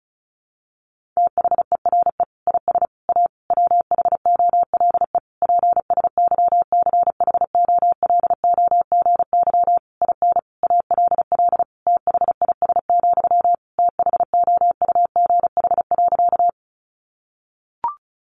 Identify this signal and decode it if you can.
Morse code, heard as a keyed tone, about 34 wpm, 700 Hz. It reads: THERE IS A WHOLE PSYCHOLOGY IN ALL THIS, THOUGH.
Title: Sentences from Top 1400 Words 35wpm